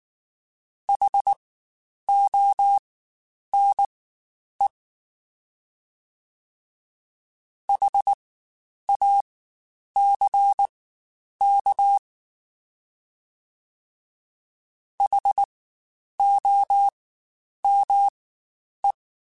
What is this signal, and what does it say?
Morse code, heard as a keyed tone, about 19 wpm, spaced as for 9 wpm, 800 Hz. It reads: HONE HACK HOME
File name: Spaced Alphas / Spaced Alphas mp3s / Hs sp @18